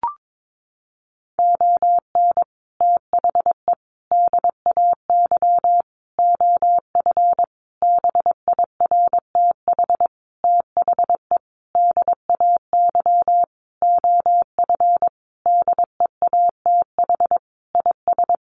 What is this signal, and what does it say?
Morse code, heard as a keyed tone, about 22 wpm, 700 Hz. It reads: ON THE DAY OF BIRTH THE DAY OF DEATH IS
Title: Sentences from Top 1100 Words 22wpm